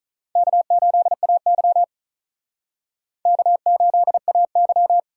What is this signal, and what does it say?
Morse code, heard as a keyed tone, about 45 wpm, 700 Hz. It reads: K8AY K8AY